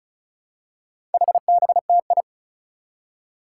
Morse code, heard as a keyed tone, about 36 wpm, 700 Hz. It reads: HBTI
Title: Four Challenging Letter-Number Combinations 35wpm